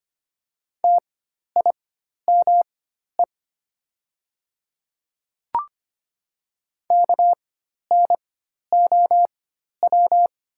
Morse code, heard as a keyed tone, about 25 wpm, spaced as for 12 wpm, 700 Hz. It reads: TIME KNOW
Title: 4x Character Spacing - Top 100 Words - 25wpm